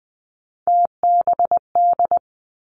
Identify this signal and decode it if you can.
Morse code, heard as a keyed tone, about 20 wpm, 700 Hz. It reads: TBD
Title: Three Challenging Letter-Number Combinations 20wpm